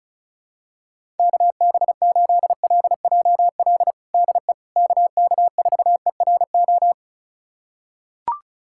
Morse code, heard as a keyed tone, about 34 wpm, 700 Hz. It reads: KB8LJL DE KK4ERO
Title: USA Callsign DE Callsign 35wpm